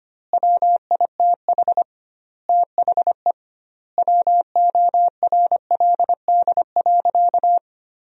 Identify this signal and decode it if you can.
Morse code, heard as a keyed tone, about 25 wpm, 700 Hz. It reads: WITH THE WORLD.
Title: 2x Word-Spacing - Sentences from Top 200 Words - 25wpm